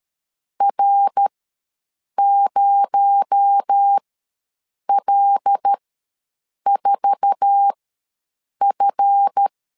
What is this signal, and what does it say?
Morse code, heard as a keyed tone, about 13 wpm, spaced as for 7 wpm, 800 Hz. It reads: R0L4F
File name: LEZIONI / 9 2 ws 12wpm 3406